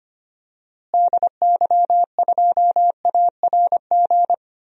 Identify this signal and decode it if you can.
Morse code, heard as a keyed tone, about 25 wpm, 700 Hz. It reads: DY2ARG
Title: DXCC-100 Call Signs - 6 Characters - Repeated 3x - 25wpm